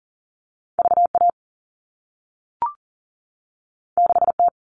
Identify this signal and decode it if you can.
Morse code, heard as a keyed tone, about 40 wpm, 700 Hz. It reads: VA 6T
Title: All Cut Numbers - 2-Digit 40wpm